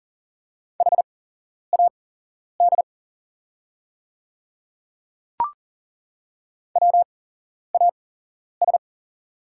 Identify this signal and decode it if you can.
Morse code, heard as a keyed tone, about 40 wpm, spaced as for 11 wpm, 700 Hz. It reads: HAD WAS
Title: 8x Character Spacing - Top 100 Words - 40wpm